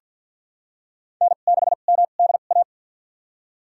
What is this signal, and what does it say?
Morse code, heard as a keyed tone, about 51 wpm, spaced as for 36 wpm, 700 Hz. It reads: N6MDA
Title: USA Call Signs - 5 Characters - 2x Character Spacing - 50wpm